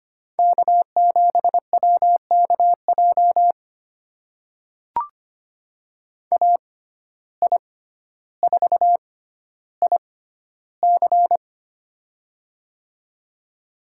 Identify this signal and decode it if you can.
Morse code, heard as a keyed tone, about 25 wpm, spaced as for 13 wpm, 700 Hz. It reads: K7WKJ AI4IC
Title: USA Call Signs - 5 Characters - 6x Character Spacing - 25wpm